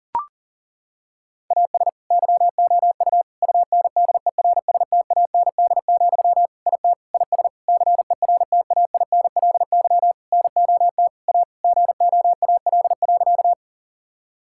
Morse code, heard as a keyed tone, about 40 wpm, 700 Hz. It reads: AS YOU UNDERSTAND, IT IS CERTAINLY NOT A GOAL.
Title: Sentences from Top 1000 Words - Review Part 5 - 40wpm